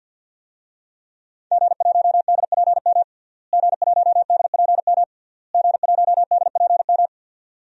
Morse code, heard as a keyed tone, about 50 wpm, 700 Hz. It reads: G1DPM G1DPM G1DPM
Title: DXCC-100 Call Signs - 5 Characters - Repeated 3x - 50wpm